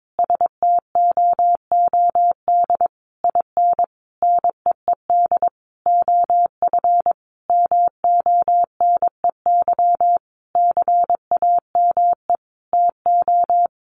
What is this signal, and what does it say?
Morse code, heard as a keyed tone, about 22 wpm, 700 Hz. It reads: STOOD IN NEED OF MONEY CAME TO
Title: Sentences from Top 1500 Words 22wpm